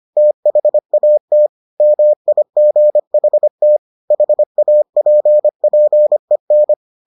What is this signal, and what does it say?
Morse code, heard as a keyed tone, about 25 wpm, 600 Hz. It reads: THAT MIGHT HAPPEN